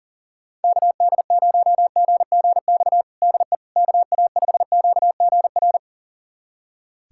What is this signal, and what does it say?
Morse code, heard as a keyed tone, about 37 wpm, 700 Hz. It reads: KD0GGX DE KA5QGR
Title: USA Callsign DE Callsign 40wpm